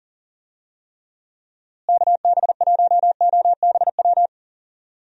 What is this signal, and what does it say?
Morse code, heard as a keyed tone, about 40 wpm, 700 Hz. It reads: KB1OBW
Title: USA Call Signs - 6 Characters - Repeated 3x - 40wpm